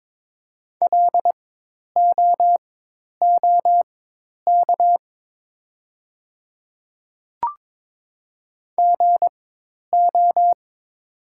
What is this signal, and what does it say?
Morse code, heard as a keyed tone, about 22 wpm, spaced as for 10 wpm, 700 Hz. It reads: LOOK GO
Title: 4x Character Spacing - Top 200 Words - 22wpm